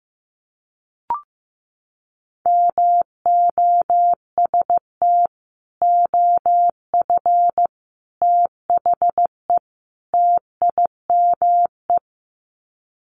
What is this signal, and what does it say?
Morse code, heard as a keyed tone, about 15 wpm, 700 Hz. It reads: MOST OF THE TIME
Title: Sets of 4 Words 15wpm